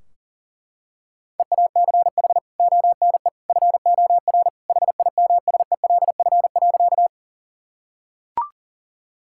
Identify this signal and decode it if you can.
Morse code, heard as a keyed tone, about 37 wpm, 700 Hz. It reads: EACH ONE FOR HIMSELF.
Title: Sentences from Top 1600 Words 40wpm